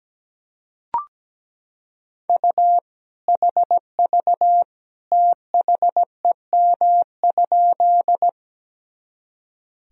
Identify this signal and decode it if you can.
Morse code, heard as a keyed tone, about 17 wpm, 700 Hz. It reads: U HV THEM?
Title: Sentences from Top 100 Words and Abbreviations 17wpm